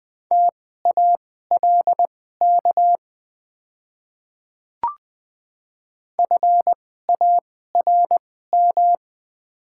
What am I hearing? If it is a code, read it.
Morse code, heard as a keyed tone, about 20 wpm, spaced as for 15 wpm, 700 Hz. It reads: TALK FARM
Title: 2x Character Spacing - Top 500 Words - 20wpm